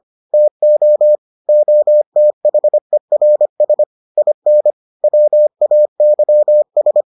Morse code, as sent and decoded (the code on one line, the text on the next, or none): - --- / --- - .... .-. ... / .. -. / .-- .- -.-- ...
TO OTHERS IN WAYS